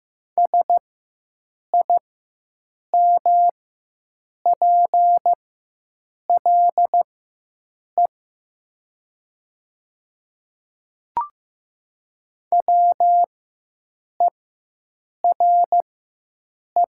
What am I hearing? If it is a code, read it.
Morse code, heard as a keyed tone, about 15 wpm, spaced as for 7 wpm, 700 Hz. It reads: SIMPLE WERE
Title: 4x Character Spacing - Top 500 Words - 15wpm